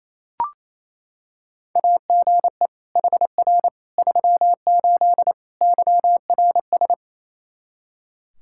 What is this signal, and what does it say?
Morse code, heard as a keyed tone, about 27 wpm, 700 Hz. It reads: AGE HR 38 YRS
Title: K1USN SST QSO Practice 28wpm